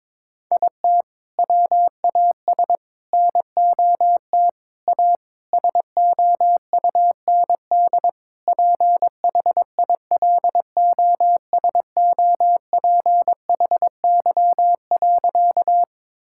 Morse code, heard as a keyed tone, about 22 wpm, 700 Hz. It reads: IT WAS NOT A SOUND PHILOSOPHY.
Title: Sentences from Top 1000 Words - Review Part 6 - 22wpm